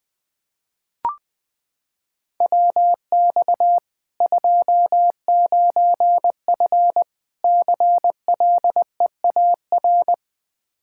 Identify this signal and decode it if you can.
Morse code, heard as a keyed tone, about 20 wpm, 700 Hz. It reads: WX 29F CLEAR
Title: Practice QSO - Exchange 2 - Info - 20wpm